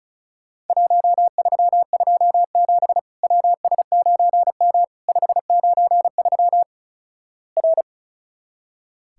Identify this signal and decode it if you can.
Morse code, heard as a keyed tone, about 36 wpm, 700 Hz. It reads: 1327 WS9M 593 R
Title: Worked All Europe DX Contest - QTC Exchange - 35wpm